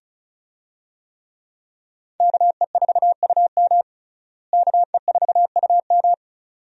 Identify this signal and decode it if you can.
Morse code, heard as a keyed tone, about 34 wpm, 700 Hz. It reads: KE4UM KE4UM
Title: USA Call Signs - 5 Characters - Repeated 2x - 35wpm